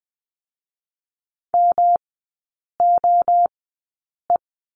Morse code, heard as a keyed tone, about 20 wpm, spaced as for 8 wpm, 690 Hz. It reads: MOE
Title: 159 Lesson 26e - Letter g - Three Characters 20wpm